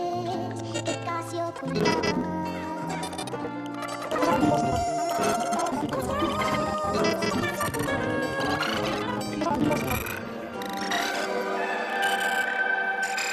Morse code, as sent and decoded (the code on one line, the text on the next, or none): none